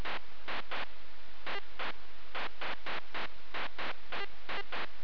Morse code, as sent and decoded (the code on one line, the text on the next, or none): none